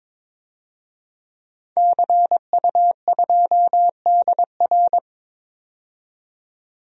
Morse code, heard as a keyed tone, about 21 wpm, 700 Hz. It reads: CU2DR
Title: DXCC-100 Call Signs - 5 Characters - 22wpm